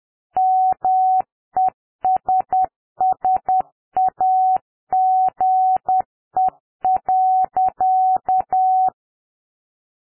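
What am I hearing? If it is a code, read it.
Morse code, heard as a keyed tone, about 10 wpm, 750 Hz. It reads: MESSAGE.
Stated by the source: Morse code generated by software with configurable tone frequency, words per minute and Farnsworth timing.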